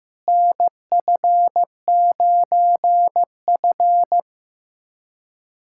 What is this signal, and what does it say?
Morse code, heard as a keyed tone, about 16 wpm, 700 Hz. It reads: NF9F
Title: USA Callsign DE Callsign - 4 Characters - 15wpm